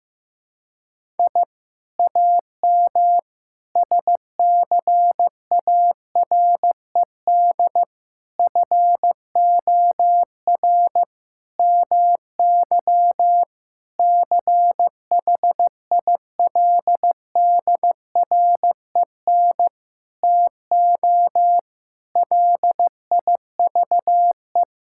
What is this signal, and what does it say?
Morse code, heard as a keyed tone, about 15 wpm, 700 Hz. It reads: I AM SCARED FOR MY CHILDREN TO LIVE
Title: Sentences from Top 1000 Words - Review Part 5 - 15wpm